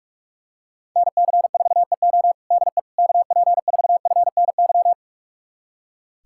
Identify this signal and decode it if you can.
Morse code, heard as a keyed tone, about 42 wpm, 700 Hz. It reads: NC4EO DE KP4FNY
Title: USA Callsign DE Callsign 45wpm